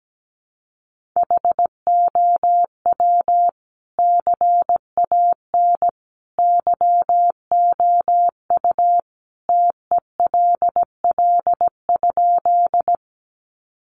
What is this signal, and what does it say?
Morse code, heard as a keyed tone, about 17 wpm, 700 Hz. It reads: HOW CAN YOU TELL?
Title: Sentences from Top 200 Words 17wpm